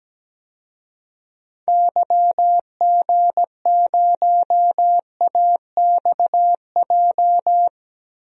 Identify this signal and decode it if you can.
Morse code, heard as a keyed tone, about 17 wpm, 700 Hz. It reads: YG0AXJ
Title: DXCC-100 Call Signs - 6 Characters - Repeated 3x - 17wpm